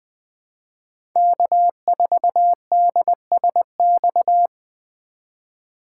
Morse code, heard as a keyed tone, about 20 wpm, 700 Hz. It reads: K4DSX
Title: 2023 POTA Activator Call Signs - 20wpm